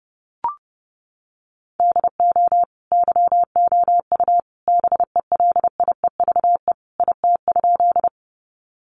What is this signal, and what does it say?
Morse code, heard as a keyed tone, about 30 wpm, 700 Hz. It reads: DO YOU BELIEVE IT?
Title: Sentences from Top 1000 Words 30wpm